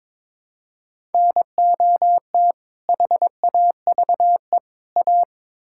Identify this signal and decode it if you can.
Morse code, heard as a keyed tone, about 22 wpm, 700 Hz. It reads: NOT HAVE A
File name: practice / 22wpm / Build A Phrase - 5-Word Phrases - 22wpm